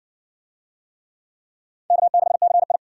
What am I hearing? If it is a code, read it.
Morse code, heard as a keyed tone, about 55 wpm, 700 Hz. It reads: K6CI